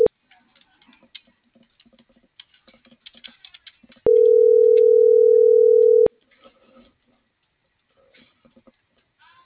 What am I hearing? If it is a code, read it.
Ringback tone (North America), 440 + 480 Hz.